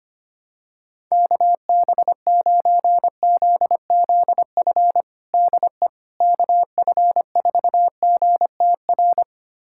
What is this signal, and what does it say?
Morse code, heard as a keyed tone, about 25 wpm, 700 Hz. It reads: KB9ZZF DE KF4GTR
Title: USA Callsign DE Callsign - 6 Characters - 25wpm